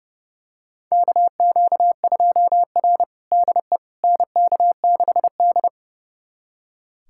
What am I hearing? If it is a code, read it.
Morse code, heard as a keyed tone, about 29 wpm, 700 Hz. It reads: KQ2R DE NK6D
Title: USA Callsign DE Callsign - 4 Characters - 30wpm